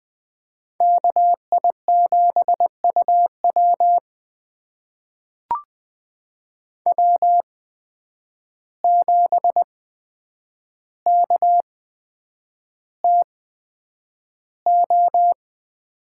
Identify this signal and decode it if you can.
Morse code, heard as a keyed tone, about 20 wpm, spaced as for 9 wpm, 700 Hz. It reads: KI7UW W7KTO
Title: USA Call Signs - 5 Characters - 8x Character Spacing - 20wpm